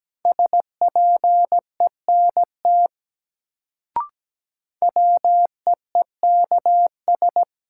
Morse code, heard as a keyed tone, about 17 wpm, 700 Hz. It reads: SPENT WEEKS
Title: Sets of 5 Words - New Word Drill - 17wpm